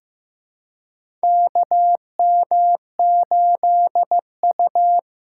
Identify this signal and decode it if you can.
Morse code, heard as a keyed tone, about 15 wpm, 700 Hz. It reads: KM8U
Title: USA Callsign DE Callsign - 4 Characters - 15wpm